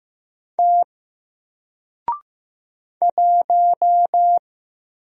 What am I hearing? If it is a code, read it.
Morse code, heard as a keyed tone, about 15 wpm, 700 Hz. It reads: T 1
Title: Single Letter-Number Alt 15wpm